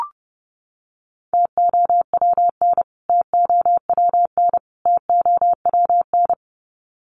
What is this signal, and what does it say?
Morse code, heard as a keyed tone, about 30 wpm, 700 Hz. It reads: TOWN TOWN TOWN
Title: Top 500 Words Repeated 30wpm